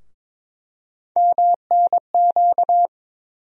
Morse code, heard as a keyed tone, about 22 wpm, 700 Hz. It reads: MNQ
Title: USA Call Sign Postfixes - 3 Letters 22wpm